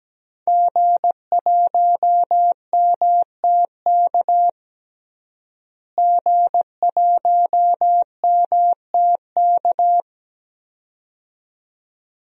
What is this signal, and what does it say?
Morse code, heard as a keyed tone, about 17 wpm, 700 Hz. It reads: G1MTK G1MTK
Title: DXCC-100 Call Signs - 5 Characters - Repeated 2x - 17wpm